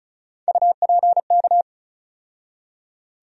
Morse code, heard as a keyed tone, about 36 wpm, 700 Hz. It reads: UPK
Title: USA Call Sign Postfixes - 3 Letters 35wpm